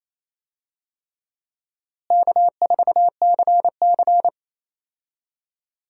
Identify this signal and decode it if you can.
Morse code, heard as a keyed tone, about 28 wpm, 700 Hz. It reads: K4CC
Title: USA Call Signs - 4 Characters - Repeated 3x - 28wpm